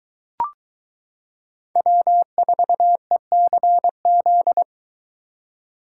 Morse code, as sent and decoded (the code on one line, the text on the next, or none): .-- ....- . -.-. --..
W4ECZ